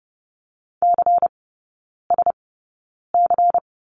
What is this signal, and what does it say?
Morse code, heard as a keyed tone, about 30 wpm, spaced as for 9 wpm, 700 Hz. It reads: CSC
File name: Lesson 16e - Letter c - Three Characters 30wpm